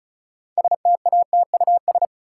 Morse code, heard as a keyed tone, about 34 wpm, 700 Hz. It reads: STATUS
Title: Sentences from Top 1400 Words - New Word Drill - 35wpm